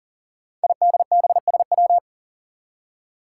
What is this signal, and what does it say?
Morse code, heard as a keyed tone, about 42 wpm, 700 Hz. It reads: IDBSW
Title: Five Challenging Letter-Number Combinations 40wpm